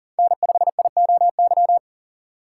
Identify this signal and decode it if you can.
Morse code, heard as a keyed tone, about 40 wpm, 700 Hz. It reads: N5IOY
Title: USA Call Signs - 5 Characters - 3x Character Spacing - 40wpm